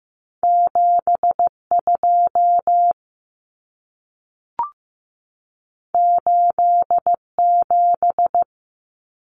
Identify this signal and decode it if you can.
Morse code, heard as a keyed tone, about 15 wpm, 700 Hz. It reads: 72 87
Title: Numbers - 2-Digit 15wpm